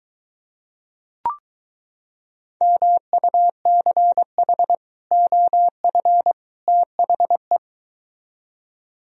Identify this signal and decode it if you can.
Morse code, heard as a keyed tone, about 23 wpm, 700 Hz. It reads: MUCH OF THE